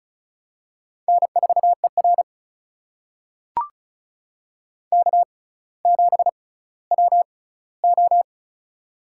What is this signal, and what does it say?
Morse code, heard as a keyed tone, about 36 wpm, spaced as for 18 wpm, 700 Hz. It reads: N4ER K7WO
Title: USA Call Signs - 4 Characters - 6x Character Spacing - 35wpm